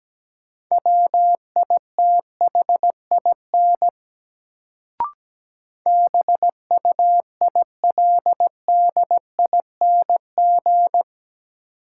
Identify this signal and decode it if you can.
Morse code, heard as a keyed tone, about 17 wpm, 700 Hz. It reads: WITHIN BUILDING
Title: Sentences from Top 700 Words - New Word Drill - 17wpm